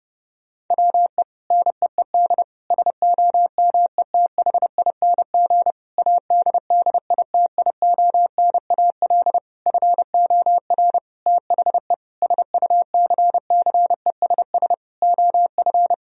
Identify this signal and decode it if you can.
Morse code, heard as a keyed tone, about 30 wpm, 700 Hz. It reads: WE NEED SOMETHING ADDITIONAL FOR THE SUCCESS OF
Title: Sentences from Top 1000 Words - Review Part 4 - 30wpm